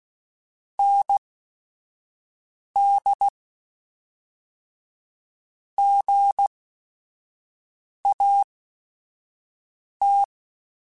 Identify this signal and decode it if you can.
Morse code, heard as a keyed tone, about 16 wpm, spaced as for 5 wpm, 800 Hz. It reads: ND GAT